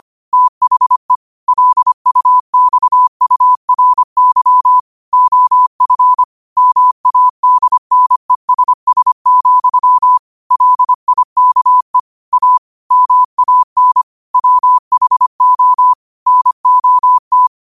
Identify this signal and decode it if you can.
Morse code, heard as a keyed tone, about 25 wpm, 1000 Hz. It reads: THE LUXURY OF MADNESS, LIKE A MAN WHO NOT
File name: qotdfiles / 25wpm